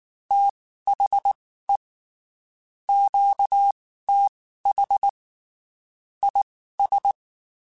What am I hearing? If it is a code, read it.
Morse code, heard as a keyed tone, about 19 wpm, spaced as for 14 wpm, 800 Hz. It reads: THE QTH IS
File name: (spaced) / QSOs Spaced-mp3s / kc7haa @18 sp